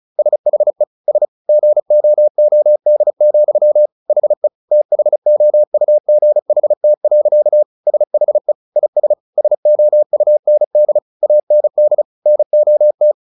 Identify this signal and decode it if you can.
Morse code, heard as a keyed tone, about 34 wpm, 600 Hz. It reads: SHE S GOOD, HE THOUGHT. SHE IS SOUND AND NOT
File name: the old man and the sea 2 35wpm